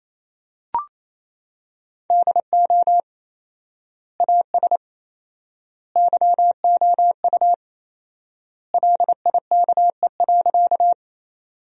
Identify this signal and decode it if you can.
Morse code, heard as a keyed tone, about 28 wpm, 700 Hz. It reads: DO AS YOU LIKE.
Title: 4x Word-Spacing - Sentences from Top 100 Words - 28wpm